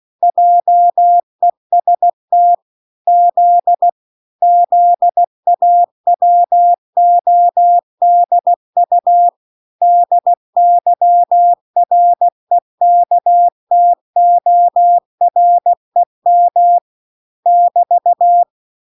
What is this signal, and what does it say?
Morse code, heard as a keyed tone, about 16 wpm, 700 Hz. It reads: JEST Z ZAWODU DYREKTOREM <BT>